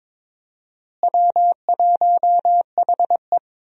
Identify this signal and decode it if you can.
Morse code, heard as a keyed tone, about 22 wpm, 700 Hz. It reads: W1HE
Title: USA Call Signs - 4 Characters - Repeated 3x - 22wpm